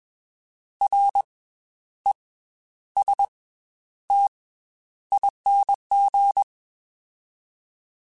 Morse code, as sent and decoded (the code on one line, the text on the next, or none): .-. / . / ... / - / .. -. --.
R E S T ING